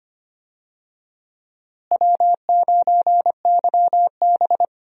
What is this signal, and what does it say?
Morse code, heard as a keyed tone, about 25 wpm, 700 Hz. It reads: W9YB
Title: USA Call Signs - 4 Characters - Repeated 2x - 25wpm